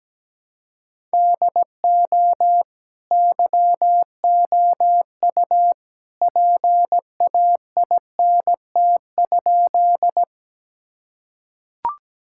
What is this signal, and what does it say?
Morse code, heard as a keyed tone, about 17 wpm, 700 Hz. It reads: DO YOU PAINT?
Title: Sentences from Top 1000 Words - Review Part 6 - 17wpm